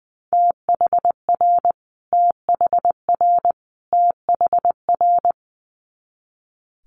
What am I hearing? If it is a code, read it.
Morse code, heard as a keyed tone, about 20 wpm, 700 Hz. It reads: THR THR THR
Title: Multi-letter Phonemes Repeated 3x 20wpm